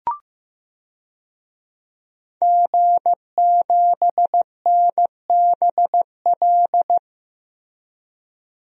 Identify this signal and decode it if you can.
Morse code, heard as a keyed tone, about 15 wpm, 700 Hz. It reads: G7NBL
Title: DXCC-100 Call Signs - 5 Characters - Repeated 2x - 15wpm